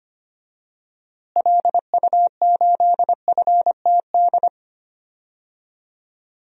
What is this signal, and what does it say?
Morse code, heard as a keyed tone, about 25 wpm, 700 Hz. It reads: LU8FTD
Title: DXCC-100 Call Signs - 6 Characters - Repeated 2x - 25wpm